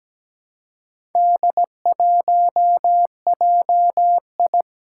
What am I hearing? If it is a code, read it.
Morse code, heard as a keyed tone, about 17 wpm, 700 Hz. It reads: D1JI
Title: Four Challenging Letter-Number Combinations 17wpm